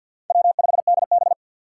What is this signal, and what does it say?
Morse code, heard as a keyed tone, about 51 wpm, 700 Hz. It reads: W5DB
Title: USA Call Signs - 4 Characters - 3x Character Spacing - 50wpm